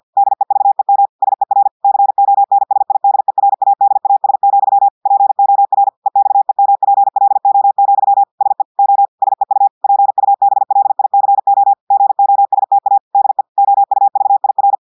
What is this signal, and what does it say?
Morse code, heard as a keyed tone, about 50 wpm, 800 Hz. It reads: DEVEM SER CONSIDERADAS, POR EXEMPLO, SE O SEU PUBLICO GOSTA DE OUVIR
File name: portugais texte en clair 2 vitesse 50 mots minutes